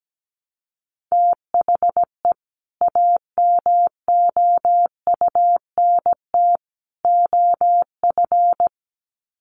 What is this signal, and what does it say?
Morse code, heard as a keyed tone, about 17 wpm, 700 Hz. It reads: THE AMOUNT OF
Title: Sets of 3 Words 17wpm